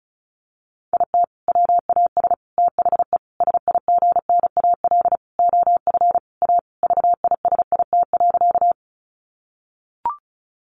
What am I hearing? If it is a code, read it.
Morse code, heard as a keyed tone, about 34 wpm, 700 Hz. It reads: IT WAS THE SIGNAL OF A VISIT.
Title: Sentences from Top 1400 Words 35wpm